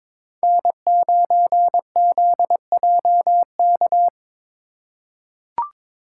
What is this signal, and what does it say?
Morse code, heard as a keyed tone, about 22 wpm, 700 Hz. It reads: N9ZJK
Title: USA Call Signs - 5 Characters - 3x Character Spacing - 22wpm